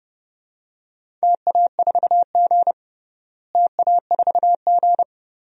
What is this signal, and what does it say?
Morse code, heard as a keyed tone, about 30 wpm, 700 Hz. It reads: TA4G TA4G
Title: DXCC-100 Call Signs - 4 Characters - Repeated 3x - 30wpm